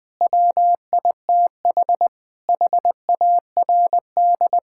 Morse code, heard as a keyed tone, about 20 wpm, 700 Hz. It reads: WITH HARD
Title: Sentences from Top 1000 Words - Review Part 5 - 20wpm